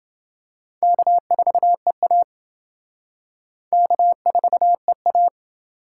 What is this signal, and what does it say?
Morse code, heard as a keyed tone, about 28 wpm, 700 Hz. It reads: K4EA K4EA